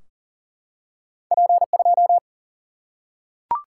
Morse code, heard as a keyed tone, about 40 wpm, 700 Hz. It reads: P2
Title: Single-Letter Single-Number 40wpm